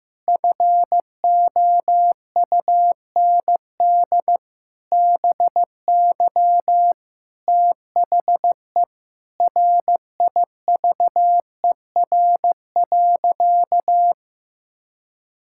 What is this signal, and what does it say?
Morse code, heard as a keyed tone, about 16 wpm, 700 Hz. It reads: FOUND BY THE RIVER.
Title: Sentences from Top 1600 Words 15wpm